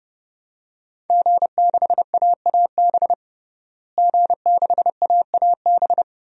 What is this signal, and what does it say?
Morse code, heard as a keyed tone, about 30 wpm, 700 Hz. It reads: G6AAB G6AAB
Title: DXCC-100 Call Signs - 5 Characters - Repeated 2x - 30wpm